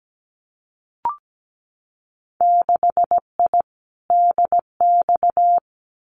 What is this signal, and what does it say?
Morse code, heard as a keyed tone, about 17 wpm, 700 Hz. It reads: 6I DX
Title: Winter Field Day Exchanges Encore 17wpm